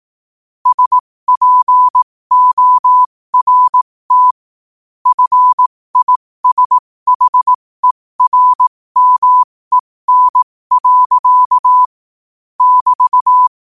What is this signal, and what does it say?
Morse code, heard as a keyed tone, about 18 wpm, spaced as for 15 wpm, 1000 Hz. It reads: SPORT FISHERMEN. <BT>
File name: qotdfiles / 15wpm